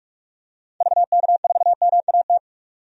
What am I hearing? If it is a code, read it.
Morse code, heard as a keyed tone, about 42 wpm, 700 Hz. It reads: VK4MAT
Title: DXCC-100 Call Signs - 6 Characters - 45wpm